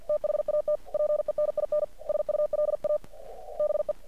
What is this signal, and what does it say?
Morse code, heard as a keyed tone, about 50 wpm, 600 Hz. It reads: THAT PENINSULA BE
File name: archived / CRSnet